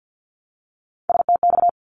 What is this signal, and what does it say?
Morse code, heard as a keyed tone, about 50 wpm, 700 Hz. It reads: STX
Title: ARRL Sections 50wpm